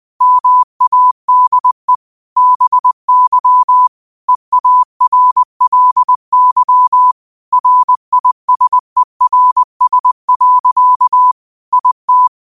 Morse code, heard as a keyed tone, about 20 wpm, 1000 Hz. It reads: MADE BY EARLY RISERS. IT
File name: qotdfiles / 20wpm